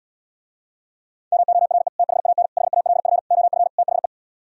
Morse code, heard as a keyed tone, about 77 wpm, 700 Hz. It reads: MORE THAN HALF OF THE